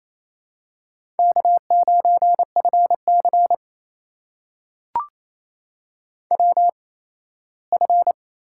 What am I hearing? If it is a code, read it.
Morse code, heard as a keyed tone, about 28 wpm, spaced as for 17 wpm, 700 Hz. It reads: K9FC WF
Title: USA Call Signs - 4 Characters - 8x Character Spacing - 28wpm